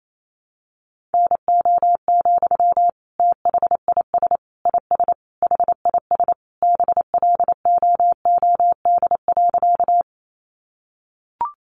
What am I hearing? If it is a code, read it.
Morse code, heard as a keyed tone, about 28 wpm, 700 Hz. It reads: NO, THIS IS HIS BLOOD.
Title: Sentences from Top 1100 Words 28wpm